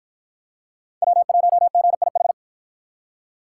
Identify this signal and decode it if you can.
Morse code, heard as a keyed tone, about 53 wpm, 700 Hz. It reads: W1GIH